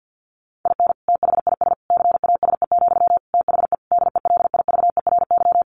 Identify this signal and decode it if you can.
Morse code, heard as a keyed tone, about 48 wpm, 700 Hz. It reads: IN THIS CASE, THE DELIVERY
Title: Sentences from Top 1500 Words 50wpm